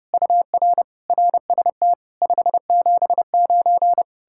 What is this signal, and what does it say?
Morse code, heard as a keyed tone, about 40 wpm, 700 Hz. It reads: UR RST 579